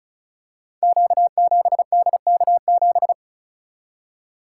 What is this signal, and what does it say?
Morse code, heard as a keyed tone, about 36 wpm, 690 Hz. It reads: Q7DK7
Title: Code Groups - 5 Letter:Numbers - 35wpm